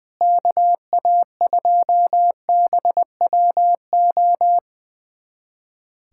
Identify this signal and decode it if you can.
Morse code, heard as a keyed tone, about 20 wpm, 700 Hz. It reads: KA2BWO
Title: USA Callsign DE Callsign - 6 Characters - 20wpm